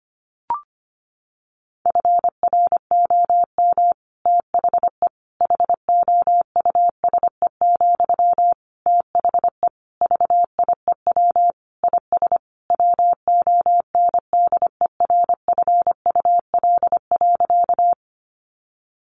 Morse code, heard as a keyed tone, about 25 wpm, 700 Hz. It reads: FROM THE HOUSE, THE VIEW IS WONDERFUL.
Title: Sentences from Top 1100 Words 25wpm